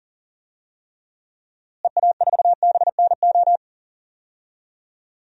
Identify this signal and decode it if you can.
Morse code, heard as a keyed tone, about 40 wpm, 700 Hz. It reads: EA4BNO
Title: DXCC-100 Call Signs - 6 Characters - Repeated 2x - 40wpm